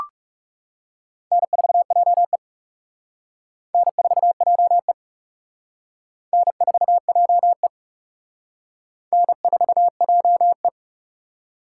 Morse code, heard as a keyed tone, about 36 wpm, 700 Hz. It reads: N4JE N4JE N4JE N4JE
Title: Speed Racing - USA 1x2 Call Signs 30wpm